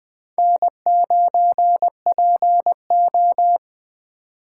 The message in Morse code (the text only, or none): N9PO